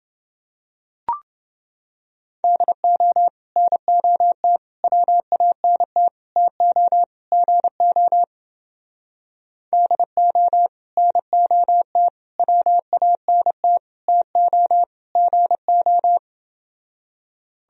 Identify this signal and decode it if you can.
Morse code, heard as a keyed tone, about 28 wpm, 700 Hz. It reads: DO NOT WANT TO GO DO NOT WANT TO GO